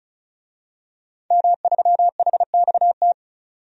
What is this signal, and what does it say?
Morse code, heard as a keyed tone, about 34 wpm, 700 Hz. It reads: M3HXT
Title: DXCC-100 Call Signs - 5 Characters - 35wpm